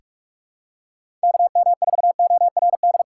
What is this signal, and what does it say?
Morse code, heard as a keyed tone, about 46 wpm, 700 Hz. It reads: KM4ORD
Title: USA Call Signs - 3x Character Spacing - 45wpm